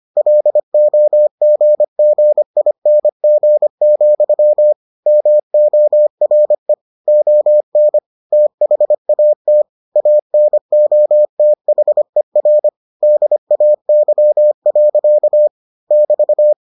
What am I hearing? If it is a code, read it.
Morse code, heard as a keyed tone, about 25 wpm, 600 Hz. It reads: LOGGING, MORE ON THAT ANOTHER DAY. <BT>